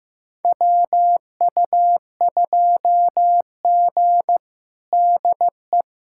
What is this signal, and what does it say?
Morse code, heard as a keyed tone, about 15 wpm, 700 Hz. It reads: WU2G DE
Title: USA Callsign DE Callsign - 4 Characters - 15wpm